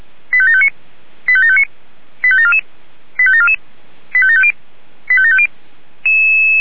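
Five-tone selective call (ZVEI), heard as quick runs of five tones, tones of about 70 ms. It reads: 85759 85759 85744 85744 85750 85750
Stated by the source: Five-tone ZVEI selective call received by SDR.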